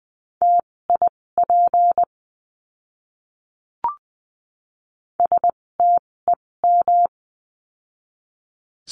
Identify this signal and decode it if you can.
Morse code, heard as a keyed tone, about 20 wpm, spaced as for 16 wpm, 700 Hz. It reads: TIP STEM
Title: 256 Lesson 42a - Words 12-20wpm